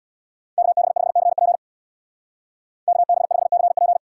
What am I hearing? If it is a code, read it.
Morse code, heard as a keyed tone, about 64 wpm, 700 Hz. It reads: KB5CP KB5CP